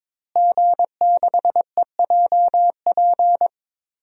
Morse code, heard as a keyed tone, about 22 wpm, 700 Hz. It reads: G6EJP
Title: DXCC-100 Call Signs - 5 Characters - Repeated 2x - 22wpm